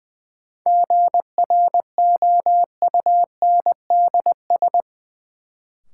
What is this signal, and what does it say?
Morse code, heard as a keyed tone, about 21 wpm, 700 Hz. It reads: GROUNDS
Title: Sentences from Top 1300 Words - New Word Drill Alt - 20wpm